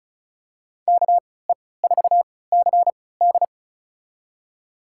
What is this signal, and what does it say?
Morse code, heard as a keyed tone, about 36 wpm, spaced as for 20 wpm, 700 Hz. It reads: KE4CD
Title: USA Call Signs - 5 Characters - 3x Character Spacing - 35wpm